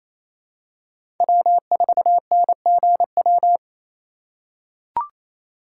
Morse code, heard as a keyed tone, about 28 wpm, 700 Hz. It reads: W4NGW
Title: USA Call Signs - 5 Characters - 2x Character Spacing - 28wpm